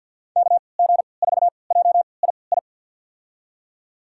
Morse code, heard as a keyed tone, about 46 wpm, spaced as for 27 wpm, 700 Hz. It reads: KG4JII
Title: USA Call Signs - 6 Characters - 3x Character Spacing - 50wpm